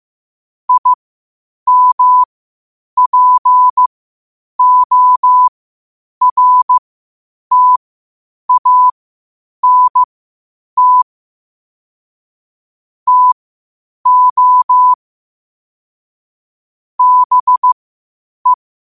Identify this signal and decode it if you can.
Morse code, heard as a keyed tone, about 15 wpm, spaced as for 8 wpm, 1000 Hz. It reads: IMPORTANT TO BE
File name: qotdfiles / 8wpm